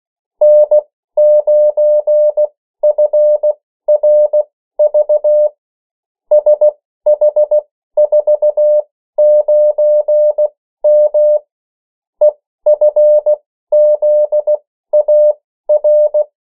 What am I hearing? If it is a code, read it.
Morse code, heard as a keyed tone, about 16 wpm, spaced as for 13 wpm, 600 Hz. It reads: N9FRV SH49M EFZAR
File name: morse mp3 / 14 urok 13